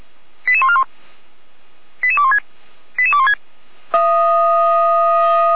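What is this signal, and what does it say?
Five-tone selective call (ZVEI), heard as quick runs of five tones, tones of about 70 ms. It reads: 88141 88216 88216
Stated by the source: ZVEI five-tone call received by SDR, logged by Feuerwehr Orlamünde (fire brigade).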